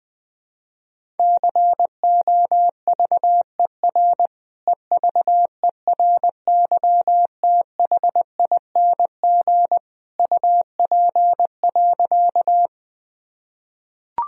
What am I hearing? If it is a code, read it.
Morse code, heard as a keyed tone, about 21 wpm, 700 Hz. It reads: COVER EVERYTHING UP.
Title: Sentences from Top 1000 Words - Review Part 1 - 20wpm